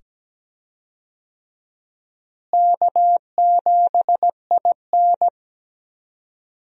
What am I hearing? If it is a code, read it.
Morse code, heard as a keyed tone, about 17 wpm, 700 Hz. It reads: K7IN
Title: USA Call Signs - 4 Characters - Repeated 2x - 17wpm